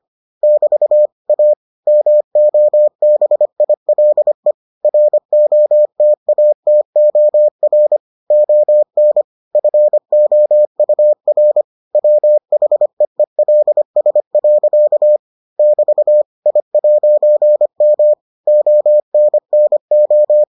Morse code, heard as a keyed tone, about 25 wpm, 600 Hz. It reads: <BT> A MOBILE ROTATOR ON FOUR WHEELS. <BT> I'M ONNO